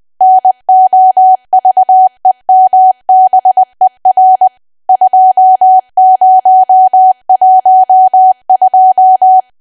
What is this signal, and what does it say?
Morse code, heard as a keyed tone, about 20 wpm, 750 Hz. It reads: NOVEMBER 2012
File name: cwa / W1AW-Level3 / 20WPM